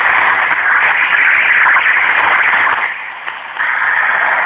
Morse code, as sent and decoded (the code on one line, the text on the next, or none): none